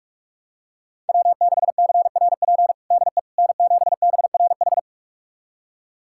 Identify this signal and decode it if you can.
Morse code, heard as a keyed tone, about 46 wpm, 700 Hz. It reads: W6CRP DE N7BRH
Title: USA Callsign DE Callsign - 5 Characters - 45wpm